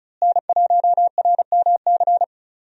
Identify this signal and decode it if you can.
Morse code, heard as a keyed tone, about 36 wpm, 700 Hz. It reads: N1RMC
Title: USA Call Signs - 5 Characters - Repeated 3x - 35wpm